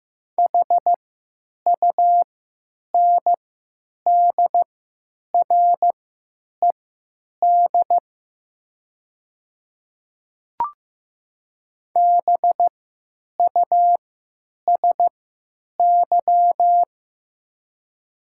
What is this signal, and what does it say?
Morse code, heard as a keyed tone, about 15 wpm, spaced as for 9 wpm, 700 Hz. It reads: HUNDRED BUSY
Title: 3x Character Spacing - Top 500 Words - 15wpm